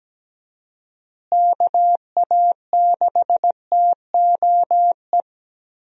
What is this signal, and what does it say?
Morse code, heard as a keyed tone, about 17 wpm, 700 Hz. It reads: KA6TOE
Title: USA Call Signs - 6 Characters - Repeated 2x - 17wpm